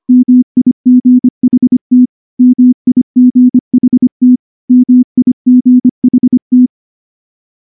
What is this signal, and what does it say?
Morse code, heard as a keyed tone, about 23 wpm, 250 Hz. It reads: MIGHT MIGHT MIGHT
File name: morse / morse-training-25WPM